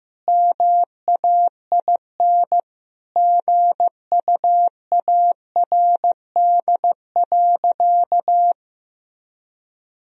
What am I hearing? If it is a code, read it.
Morse code, heard as a keyed tone, about 15 wpm, 700 Hz. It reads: MAIN GUARD.
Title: Sentences from Top 1000 Words - Review Part 4 - 15wpm